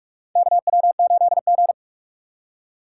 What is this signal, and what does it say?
Morse code, heard as a keyed tone, about 44 wpm, 700 Hz. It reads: KW8G
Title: USA Call Signs - 4 Characters - 2x Character Spacing - 45wpm